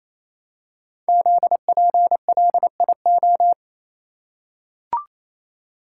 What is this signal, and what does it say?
Morse code, heard as a keyed tone, about 29 wpm, 700 Hz. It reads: ZPLIO
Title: Code Groups - 5 Letters - 28wpm